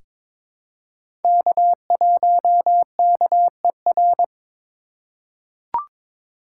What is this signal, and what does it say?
Morse code, heard as a keyed tone, about 22 wpm, 700 Hz. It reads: K1KER
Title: USA Call Signs - 4x Character Spacing - 22wpm